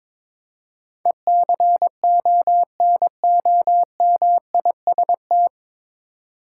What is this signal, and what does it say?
Morse code, heard as a keyed tone, about 22 wpm, 700 Hz. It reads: ECONOMIST
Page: Morse Code Ninja practice 22wpm